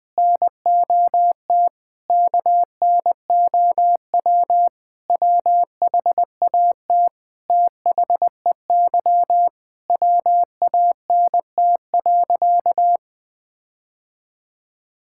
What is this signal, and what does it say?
Morse code, heard as a keyed tone, about 20 wpm, 700 Hz. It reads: NOT KNOW WHAT THEY WANT.
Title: Sentences with 5-Word Phrases - Spoken Beginning - 20wpm